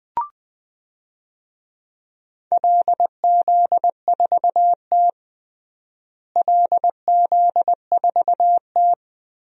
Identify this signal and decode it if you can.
Morse code, heard as a keyed tone, about 20 wpm, 700 Hz. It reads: LZ4T LZ4T
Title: DXCC-100 Call Signs - 4 Characters - Repeated 3x - 20wpm